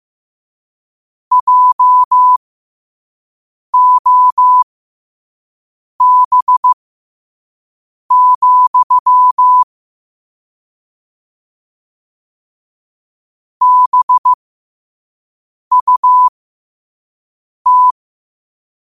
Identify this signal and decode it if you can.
Morse code, heard as a keyed tone, about 15 wpm, spaced as for 5 wpm, 1000 Hz. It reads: JOB, BUT
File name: 5wpm